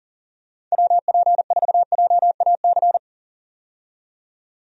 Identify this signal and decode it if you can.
Morse code, heard as a keyed tone, about 37 wpm, 700 Hz. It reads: WP4JAC
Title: USA Call Signs - 6 Characters - 40wpm